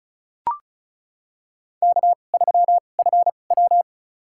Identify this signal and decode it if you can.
Morse code, heard as a keyed tone, about 36 wpm, spaced as for 25 wpm, 700 Hz. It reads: K3FW
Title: USA Call Signs - 4 Characters - 2x Character Spacing - 35wpm